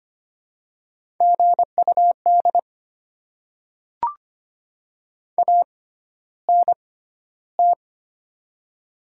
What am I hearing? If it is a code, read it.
Morse code, heard as a keyed tone, about 25 wpm, spaced as for 13 wpm, 700 Hz. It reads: GUD ANT
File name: Top 20 QSO Elements - 6x Character Spacing - 25wpm